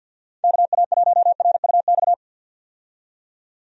Morse code, heard as a keyed tone, about 51 wpm, 700 Hz. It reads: KA1RUX
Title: USA Call Signs - 6 Characters - 50wpm